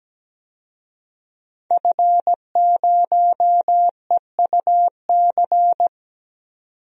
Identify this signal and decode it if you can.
Morse code, heard as a keyed tone, about 17 wpm, 700 Hz. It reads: F0EUC
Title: DXCC-100 Call Signs - 5 Characters - 17wpm